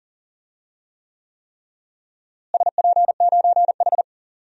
Morse code, heard as a keyed tone, about 40 wpm, 700 Hz. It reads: SP9H
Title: DXCC-100 Call Signs - 4 Characters - Repeated 3x - 40wpm